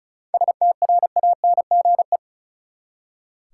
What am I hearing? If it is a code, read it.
Morse code, heard as a keyed tone, about 34 wpm, 700 Hz. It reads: STRANGE
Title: Seven Letter Words - 35wpm